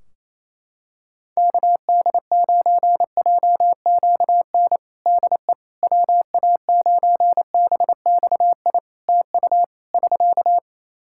Morse code, heard as a keyed tone, about 28 wpm, 700 Hz. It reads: KD9JQN DE WA9BXI TU <SK>
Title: Practice QSO - Exchange 4 - Ending - 28wpm